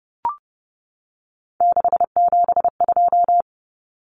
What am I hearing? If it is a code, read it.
Morse code, heard as a keyed tone, about 30 wpm, 700 Hz. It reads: 672
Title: Numbers - 3-Digit 30wpm